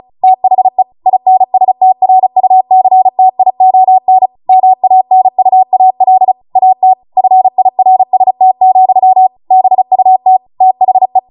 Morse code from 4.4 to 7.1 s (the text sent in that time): MANUAL AT